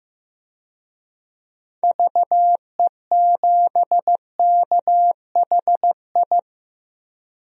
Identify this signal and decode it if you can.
Morse code, heard as a keyed tone, about 15 wpm, 700 Hz. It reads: VE7KHI
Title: DXCC-100 Call Signs - 6 Characters - 15wpm